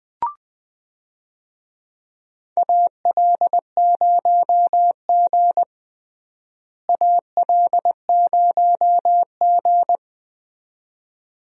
Morse code, heard as a keyed tone, about 20 wpm, 700 Hz. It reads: AL0G AL0G
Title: USA Call Signs - 4 Characters - Repeated 2x - 20wpm